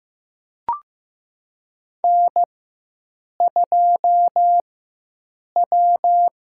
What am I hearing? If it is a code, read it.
Morse code, heard as a keyed tone, about 15 wpm, 700 Hz. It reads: N 2 W
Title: USA Call Signs - 4 Characters - 4x Character Spacing - 15wpm